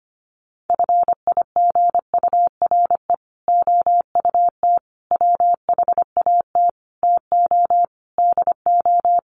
Morse code, heard as a keyed tone, about 25 wpm, 700 Hz. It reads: FIGURE OUT WHAT TO DO
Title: Sets of 5 Words Encore2 25wpm